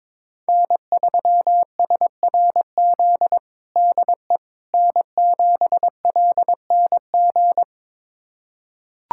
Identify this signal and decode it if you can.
Morse code, heard as a keyed tone, about 22 wpm, 700 Hz. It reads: N3SRZ DE N7LNG
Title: USA Callsign DE Callsign - 5 Characters - 22wpm